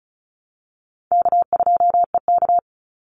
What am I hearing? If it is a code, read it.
Morse code, heard as a keyed tone, about 34 wpm, 700 Hz. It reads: K2EK
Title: Easy Call Signs Repeated 2x - 35wpm